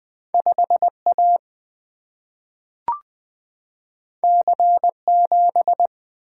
Morse code, heard as a keyed tone, about 20 wpm, 700 Hz. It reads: 5A C7
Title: Single-Letter Single-Number 20wpm